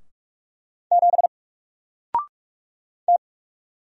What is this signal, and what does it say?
Morse code, heard as a keyed tone, about 44 wpm, 700 Hz. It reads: 7 T
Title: Single Character (Letter:Number:Punctuation:Prosign) 45wpm